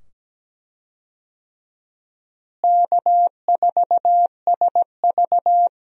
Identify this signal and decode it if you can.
Morse code, heard as a keyed tone, about 17 wpm, 700 Hz. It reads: K4SV
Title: USA Call Signs - 4 Characters - Repeated 2x - 17wpm